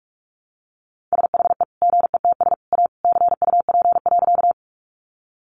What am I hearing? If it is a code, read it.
Morse code, heard as a keyed tone, about 44 wpm, 700 Hz. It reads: SHE GETS A CUP.
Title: Additional Easy Phrases 45wpm